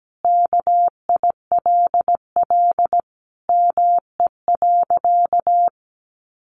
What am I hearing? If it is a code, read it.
Morse code, heard as a keyed tone, about 17 wpm, 700 Hz. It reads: KILL ME.
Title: Sentences from Top 1400 Words 17wpm